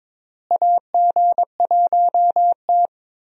Morse code, heard as a keyed tone, about 22 wpm, 700 Hz. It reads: AG1T
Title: USA Call Signs - 4 Characters - Repeated 3x - 22wpm